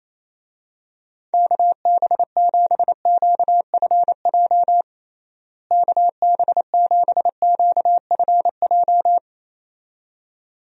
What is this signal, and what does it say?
Morse code, heard as a keyed tone, about 28 wpm, 700 Hz. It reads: KB7QFJ KB7QFJ
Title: USA Call Signs - 6 Characters - Repeated 2x - 28wpm